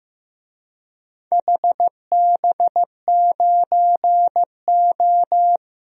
Morse code, heard as a keyed tone, about 15 wpm, 700 Hz. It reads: HB9O
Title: DXCC-100 Call Signs - 4 Characters - Repeated 2x - 15wpm